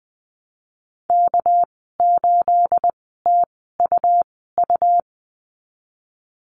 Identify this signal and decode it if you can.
Morse code, heard as a keyed tone, about 20 wpm, spaced as for 14 wpm, 700 Hz. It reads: K8TUU
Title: 213 Lesson 34g - Number 8 - Call Signs 20wpm